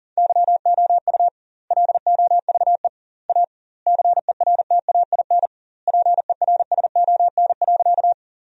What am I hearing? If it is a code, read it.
Morse code, heard as a keyed tone, about 40 wpm, 700 Hz. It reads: YOU LOVE A CERTAIN PERSON.
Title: 2x Word-Spacing - Sentences from Top 500 Words - 40wpm